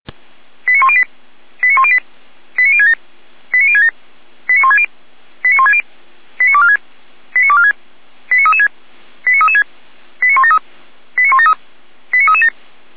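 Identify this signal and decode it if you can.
Five-tone selective call (ZVEI), heard as quick runs of five tones, tones of about 70 ms. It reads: 89108 89108 89067 89067 89160 89160 89246 89246 89336 89336 89173 89173 89337
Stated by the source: ZVEI five-tone call received on SDR.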